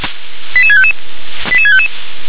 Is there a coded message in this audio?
Yes: five-tone selective call (ZVEI) from 0.5 to 1.9 s, heard as quick runs of five tones, tones of about 70 ms. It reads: 88655 88655